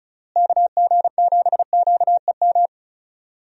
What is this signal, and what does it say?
Morse code, heard as a keyed tone, about 36 wpm, 700 Hz. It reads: KG7QEM
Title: USA Call Signs - 6 Characters - Repeated 3x - 35wpm